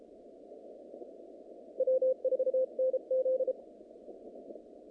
Morse code, heard as a keyed tone, about 34 wpm, 520 Hz. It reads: W4NZ
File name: contests / naqp-cw-k5tr-op-k5pi- -2100